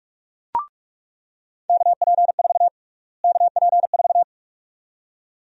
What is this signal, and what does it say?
Morse code, heard as a keyed tone, about 46 wpm, 700 Hz. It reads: KP4 KP4
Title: USA Call Sign Prefixes - Repeated 2x - 45wpm